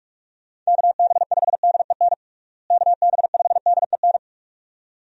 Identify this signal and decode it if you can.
Morse code, heard as a keyed tone, about 46 wpm, 700 Hz. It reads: KB5DEN KB5DEN
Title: USA Call Signs - 6 Characters - Repeated 2x - 45wpm